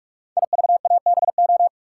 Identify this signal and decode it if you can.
Morse code, heard as a keyed tone, about 44 wpm, 700 Hz. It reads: IVABO
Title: Code Groups - 5 Letters - 45wpm